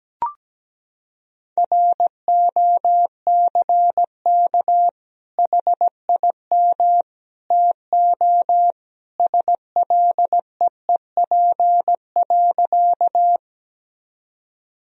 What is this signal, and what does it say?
Morse code, heard as a keyed tone, about 17 wpm, 700 Hz. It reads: ROCK HIM TO SLEEP.
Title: Sentences from Top 1000 Words - Review Part 2 - 17wpm